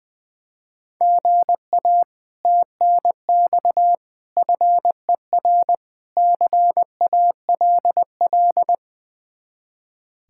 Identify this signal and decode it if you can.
Morse code, heard as a keyed tone, about 20 wpm, 700 Hz. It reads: GA TNX FER CALL
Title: Speed-Racing Common QSO Phrases - 20wpm